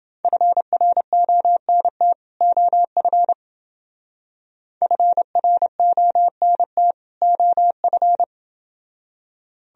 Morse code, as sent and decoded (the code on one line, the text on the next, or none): ..-. .-. --- -. - / --- ..-. / ..-. .-. --- -. - / --- ..-.
FRONT OF FRONT OF